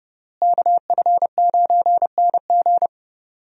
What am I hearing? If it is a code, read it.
Morse code, heard as a keyed tone, about 30 wpm, 700 Hz. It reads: KF9NG
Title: USA Call Signs - 5 Characters - Repeated 3x - 30wpm